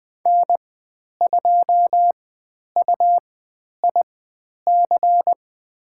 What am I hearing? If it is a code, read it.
Morse code, heard as a keyed tone, about 20 wpm, spaced as for 10 wpm, 700 Hz. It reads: N2UIC
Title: 102 Lesson 17g - Letter u - Call Signs 20wpm